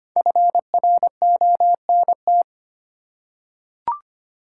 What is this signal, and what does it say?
Morse code, heard as a keyed tone, about 25 wpm, 700 Hz. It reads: FRONT
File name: 5 Letter Words - First 3 Letters Given - 25wpm